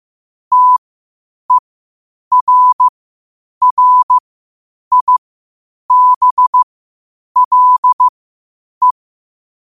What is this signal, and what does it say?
Morse code, heard as a keyed tone, about 15 wpm, spaced as for 8 wpm, 1000 Hz. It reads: TERRIBLE
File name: qotdfiles / 8wpm